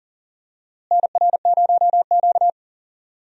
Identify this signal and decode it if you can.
Morse code, heard as a keyed tone, about 40 wpm, 700 Hz. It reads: NR0Q
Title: USA Call Signs - 4 Characters - Repeated 2x - 40wpm